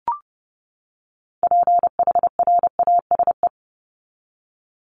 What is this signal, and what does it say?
Morse code, heard as a keyed tone, about 30 wpm, 700 Hz. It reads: PHRASE
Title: Top 1501-1750 Words 30wpm